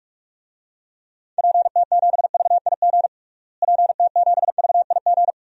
Morse code, heard as a keyed tone, about 46 wpm, 700 Hz. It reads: PT7VIG PT7VIG
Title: DXCC-100 Call Signs - 6 Characters - Repeated 3x - 45wpm